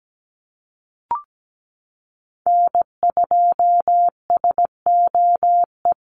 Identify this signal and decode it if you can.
Morse code, heard as a keyed tone, about 17 wpm, 700 Hz. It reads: N2SOE
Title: USA Call Signs Repeated 2x 17wpm